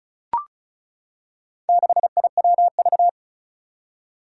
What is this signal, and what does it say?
Morse code, heard as a keyed tone, about 36 wpm, 690 Hz. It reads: 6IWV
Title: Four Challenging Letter-Number Combinations 35wpm